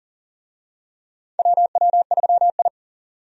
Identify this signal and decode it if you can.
Morse code, heard as a keyed tone, about 40 wpm, 700 Hz. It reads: WW3I
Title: USA Call Signs - 4 Characters - 40wpm